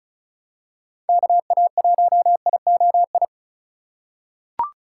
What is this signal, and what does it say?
Morse code, heard as a keyed tone, about 34 wpm, 690 Hz. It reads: KA1IOI
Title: USA Call Signs - 6 Characters - 6x Character Spacing - 35wpm